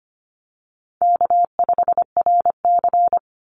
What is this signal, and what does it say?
Morse code, heard as a keyed tone, about 25 wpm, 700 Hz. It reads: K5RC
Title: Easy Call Signs - 25wpm